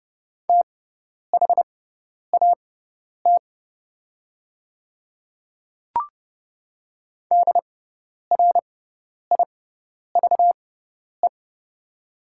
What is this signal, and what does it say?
Morse code, heard as a keyed tone, about 31 wpm, spaced as for 10 wpm, 700 Hz. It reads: THAT DRIVE
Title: 6x Character Spacing - Top 500 Words - 30wpm